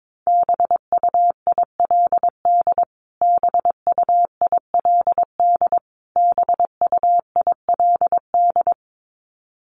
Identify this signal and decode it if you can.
Morse code, heard as a keyed tone, about 22 wpm, 700 Hz. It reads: BUILD BUILD BUILD
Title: Top 500 Words Repeated 22wpm